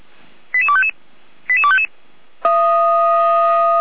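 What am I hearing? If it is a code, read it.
Five-tone selective call (ZVEI), heard as quick runs of five tones, tones of about 70 ms. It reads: 88255 88255
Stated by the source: ZVEI five-tone selective call 88251 received by SDR.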